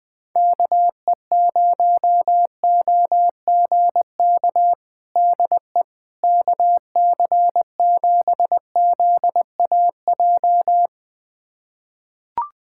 Morse code, heard as a keyed tone, about 20 wpm, 700 Hz. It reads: KE0OGK DE KC7ZAJ
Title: USA Callsign DE Callsign - 6 Characters - 20wpm